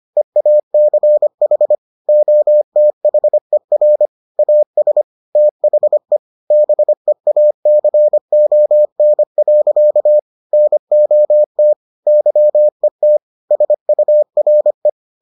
Morse code, heard as a keyed tone, about 25 wpm, 600 Hz. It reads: EACH OTHER AS THE BEACON. NOT YET SURE